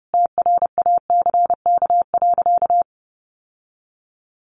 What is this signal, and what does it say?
Morse code, heard as a keyed tone, about 30 wpm, 700 Hz. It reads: TRACK.
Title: Sentences from Top 1100 Words 30wpm